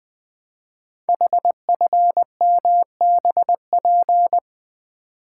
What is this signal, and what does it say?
Morse code, heard as a keyed tone, about 20 wpm, 700 Hz. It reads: HFMBP
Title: Code Groups - 5 Letter:Numbers - 20wpm